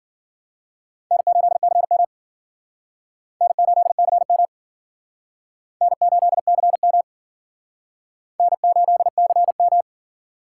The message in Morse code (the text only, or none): N8CM N8CM N8CM N8CM